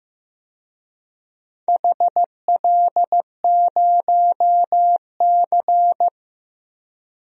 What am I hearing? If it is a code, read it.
Morse code, heard as a keyed tone, about 15 wpm, 700 Hz. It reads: HL0C
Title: DXCC-100 Call Signs - 4 Characters - Repeated 2x - 15wpm